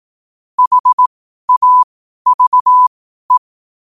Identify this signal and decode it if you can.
Morse code, heard as a keyed tone, about 18 wpm, spaced as for 12 wpm, 1000 Hz. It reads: HAVE
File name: qotdfiles / 12wpm